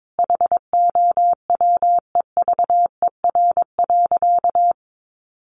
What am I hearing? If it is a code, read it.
Morse code, heard as a keyed tone, about 22 wpm, 700 Hz. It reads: HOWEVER.
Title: Sentences from Top 800 Words 22wpm